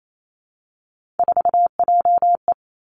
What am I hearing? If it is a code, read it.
Morse code, heard as a keyed tone, about 28 wpm, 690 Hz. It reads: 4JE
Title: Three Challenging Letter-Number Combinations 28wpm